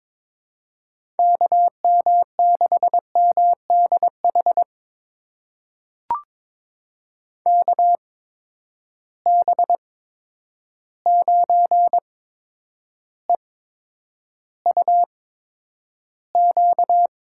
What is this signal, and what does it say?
Morse code, heard as a keyed tone, about 22 wpm, spaced as for 9 wpm, 700 Hz. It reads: KM6MDH KB9EUQ
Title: USA Call Signs - 8x Character Spacing - 22wpm